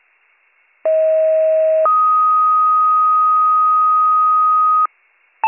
Two-tone page, heard as a one-second tone, then a three-second tone, about 640 then 1250 Hz. It is heard from 0.8 to 4.9 s.